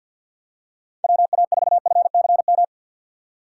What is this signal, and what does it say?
Morse code, heard as a keyed tone, about 54 wpm, 700 Hz. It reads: WA4FCM
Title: USA Call Signs - 6 Characters - 8x Character Spacing - 50wpm